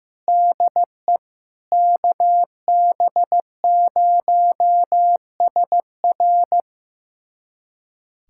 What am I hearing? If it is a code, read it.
Morse code, heard as a keyed tone, about 15 wpm, 700 Hz. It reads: DE KB0SR
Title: USA Callsign DE Callsign - 5 Characters - 15wpm